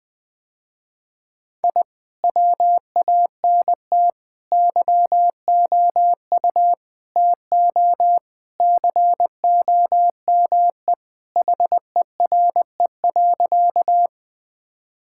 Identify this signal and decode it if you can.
Morse code, heard as a keyed tone, about 19 wpm, 700 Hz. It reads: I WANT YOU TO COME HERE.
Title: Sentences with 2-Word Phrases - Spoken Beginning - 20wpm